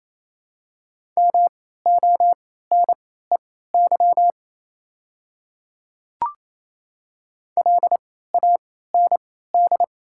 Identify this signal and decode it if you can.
Morse code, heard as a keyed tone, about 28 wpm, spaced as for 16 wpm, 690 Hz. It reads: MONEY LAND
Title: 3x Character Spacing - Top 500 Words - 28wpm